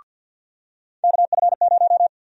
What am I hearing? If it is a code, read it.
Morse code, heard as a keyed tone, about 51 wpm, 700 Hz. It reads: KF0
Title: USA Call Sign Prefixes - 50wpm